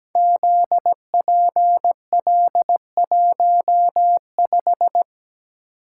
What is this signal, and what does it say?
Morse code, heard as a keyed tone, about 17 wpm, 700 Hz. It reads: ZPL15
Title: Code Groups - 5 Letter:Numbers - 17wpm